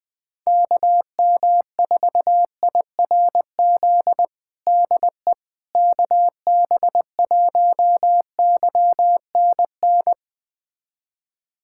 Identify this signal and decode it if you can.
Morse code, heard as a keyed tone, about 20 wpm, 700 Hz. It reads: KM4IRZ DE KB1YNN
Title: USA Callsign DE Callsign - 6 Characters - 20wpm